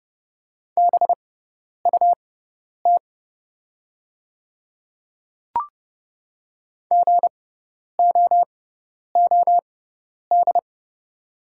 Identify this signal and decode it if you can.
Morse code, heard as a keyed tone, about 29 wpm, spaced as for 10 wpm, 700 Hz. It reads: BUT GOOD
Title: 6x Character Spacing - Top 300 Words - 30wpm